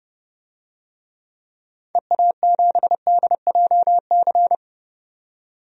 Morse code, heard as a keyed tone, about 30 wpm, 700 Hz. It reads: EA7DJC
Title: DXCC-100 Call Signs - 6 Characters - 30wpm